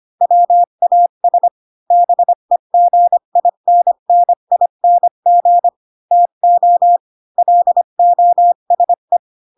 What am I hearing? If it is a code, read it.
Morse code, heard as a keyed tone, about 25 wpm, 700 Hz. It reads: WAS BEGINNING TO LOSE
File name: Pinocchio / Pinocchio ch4 20 25 ik1hge